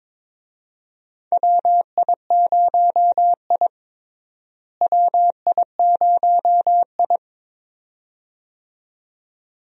Morse code, heard as a keyed tone, about 22 wpm, 700 Hz. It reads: WI0I WI0I
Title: USA Call Signs - 4 Characters - Repeated 2x - 22wpm